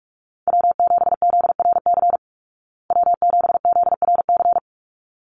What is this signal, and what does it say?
Morse code, heard as a keyed tone, about 44 wpm, 700 Hz. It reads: W7ZRC W7ZRC
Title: Call Signs Repeated 3x - 45wpm